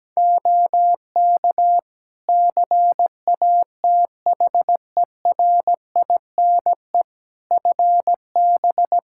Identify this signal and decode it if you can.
Morse code, heard as a keyed tone, about 17 wpm, 700 Hz. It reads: OK CATHERINE FB
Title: Practice QSO - Exchange 2 - Info - 17wpm